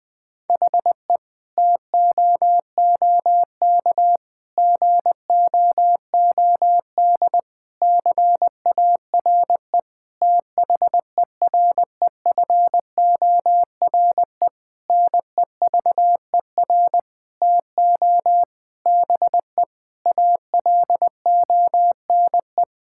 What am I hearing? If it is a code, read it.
Morse code, heard as a keyed tone, about 20 wpm, 700 Hz. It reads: HE TOOK GOOD CARE THEREFORE NEVER TO BE ALONE
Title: Sentences from Top 1000 Words - Review Part 5 - 20wpm